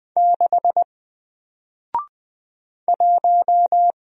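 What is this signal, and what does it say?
Morse code, heard as a keyed tone, about 20 wpm, 700 Hz. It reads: 6 1
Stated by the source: Single letter and number Morse drill at 20 wpm, from N4LCD.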